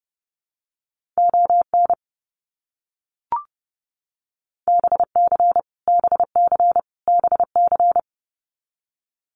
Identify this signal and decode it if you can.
Morse code, heard as a keyed tone, about 30 wpm, 700 Hz. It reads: ON BC BC BC
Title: Canadian Provinces and Territories - Repeated 3x - 30wpm